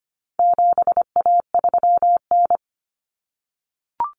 Morse code, heard as a keyed tone, about 25 wpm, 700 Hz. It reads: 7A3N
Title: Common Cut Numbers - 4-Digit 25wpm